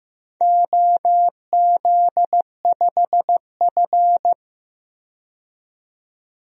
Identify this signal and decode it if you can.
Morse code, heard as a keyed tone, about 15 wpm, 700 Hz. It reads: OZ5F
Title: DXCC-100 Call Signs - 4 Characters - 15wpm